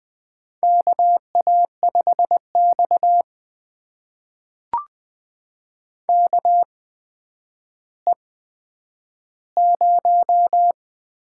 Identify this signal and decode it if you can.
Morse code, heard as a keyed tone, about 20 wpm, spaced as for 10 wpm, 700 Hz. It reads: KA5X KE0
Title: USA Call Signs - 8x Character Spacing - 20wpm